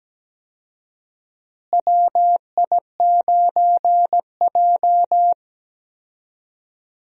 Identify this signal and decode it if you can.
Morse code, heard as a keyed tone, about 17 wpm, 700 Hz. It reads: WI9J
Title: USA Call Signs - 4 Characters - Repeated 3x - 17wpm